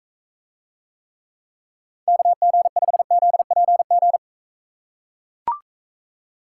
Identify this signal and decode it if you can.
Morse code, heard as a keyed tone, about 40 wpm, 700 Hz. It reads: KG5ZPG